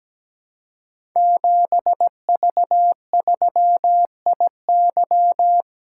Morse code, heard as a keyed tone, about 17 wpm, 700 Hz. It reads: 7V3IY
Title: Code Groups - 5 Letter:Numbers - 17wpm